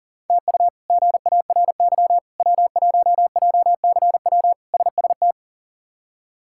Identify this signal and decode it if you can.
Morse code, heard as a keyed tone, about 40 wpm, 700 Hz. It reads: TU GARY W1JCW SST
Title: K1USN SST QSO Practice - 40wpm